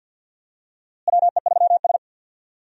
Morse code, heard as a keyed tone, about 47 wpm, 700 Hz. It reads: WE3S
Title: USA Call Signs - 4 Characters - Repeated 2x - 50wpm